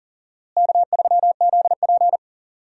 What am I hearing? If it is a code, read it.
Morse code, heard as a keyed tone, about 40 wpm, 700 Hz. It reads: K3ZP
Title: USA Call Signs - 4 Characters - Repeated 3x - 40wpm